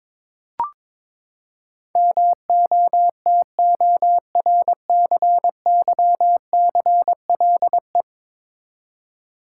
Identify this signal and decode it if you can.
Morse code, heard as a keyed tone, about 21 wpm, 700 Hz. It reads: MOTORCYCLE
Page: morse-code-ninja-practice-22wpm